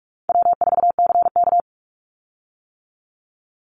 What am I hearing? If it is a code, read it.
Morse code, heard as a keyed tone, about 46 wpm, 700 Hz. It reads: W4CK
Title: Easy Call Signs - 45wpm